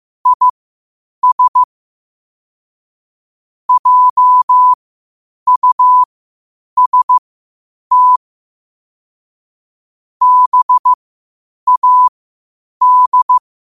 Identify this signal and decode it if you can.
Morse code, heard as a keyed tone, about 15 wpm, spaced as for 8 wpm, 1000 Hz. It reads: IS JUST BAD
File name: qotdfiles / 8wpm